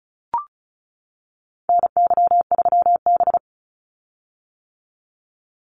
Morse code, heard as a keyed tone, about 34 wpm, 700 Hz. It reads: NY3B
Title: Call Signs - 35wpm